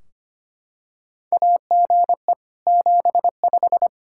Morse code, heard as a keyed tone, about 25 wpm, 700 Hz. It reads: AGE 75
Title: K1USN SST QSO Practice 25wpm